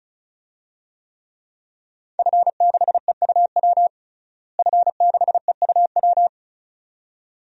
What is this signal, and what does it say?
Morse code, heard as a keyed tone, about 34 wpm, 700 Hz. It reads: F6EUW F6EUW
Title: DXCC-100 Call Signs - 5 Characters - Repeated 2x - 35wpm